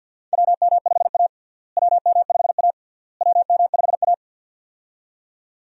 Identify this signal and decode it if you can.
Morse code, heard as a keyed tone, about 52 wpm, 700 Hz. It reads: WM5A WM5A WM5A
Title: USA Call Signs - 4 Characters - Repeated 3x - 50wpm